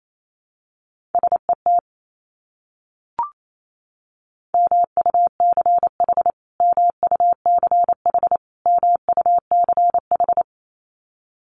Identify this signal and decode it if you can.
Morse code, heard as a keyed tone, about 28 wpm, 700 Hz. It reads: SET MUCH MUCH MUCH
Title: Top 200 Words Repeated 28wpm